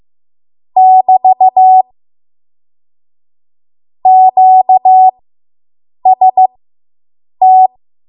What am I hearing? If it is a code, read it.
Morse code, heard as a keyed tone, about 15 wpm, spaced as for 7 wpm, 750 Hz. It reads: <BT> QST